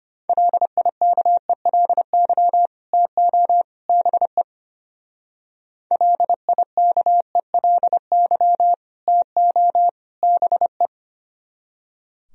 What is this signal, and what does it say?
Morse code, heard as a keyed tone, about 29 wpm, 700 Hz. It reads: LIKELY TO BE LIKELY TO BE
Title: Speed-Racing - Sets of 3 Words 25wpm